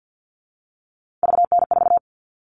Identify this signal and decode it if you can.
Morse code, heard as a keyed tone, about 50 wpm, 700 Hz. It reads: VN4
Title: All Cut Numbers - 3-Digit 50wpm